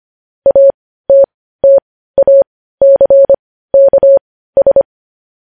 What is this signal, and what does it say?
Morse code, heard as a keyed tone, about 25 wpm, spaced as for 15 wpm, 550 Hz. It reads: ATTACKS
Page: News Headlines in Morse Code at 15 WPM